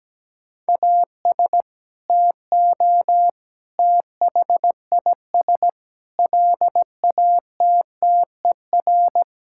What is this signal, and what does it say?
Morse code, heard as a keyed tone, about 17 wpm, 700 Hz. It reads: AS TO THIS LATTER
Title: Sentences from Top 1600 Words 17wpm